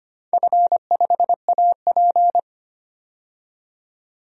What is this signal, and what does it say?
Morse code, heard as a keyed tone, about 25 wpm, 700 Hz. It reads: F5AP
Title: DXCC-100 Call Signs - 4 Characters - 25wpm